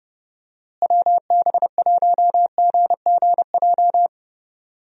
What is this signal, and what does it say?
Morse code, heard as a keyed tone, about 30 wpm, 700 Hz. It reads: WB1GGJ
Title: USA Call Signs - 6 Characters - Repeated 2x - 30wpm